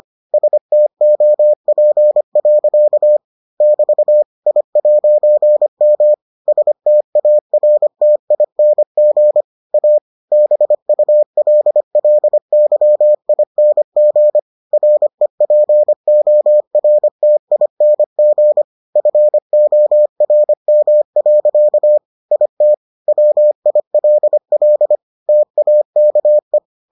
Morse code, heard as a keyed tone, about 25 wpm, 600 Hz. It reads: STOP. <BT> I'M STARTING A BULLYING REPORTING FORM. IT WILL TAKE